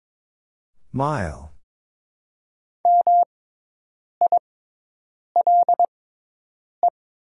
Morse code, heard as a keyed tone, about 22 wpm, spaced as for 7 wpm, 700 Hz. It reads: EMILE
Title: 6x Character Spacing - Top 500 Words - 22wpm